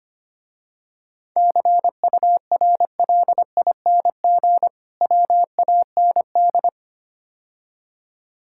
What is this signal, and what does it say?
Morse code, heard as a keyed tone, about 25 wpm, 700 Hz. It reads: CURLING WAND
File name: practice / 25wpm / Difficult Words - Speed-Racing with Hints - 25wpm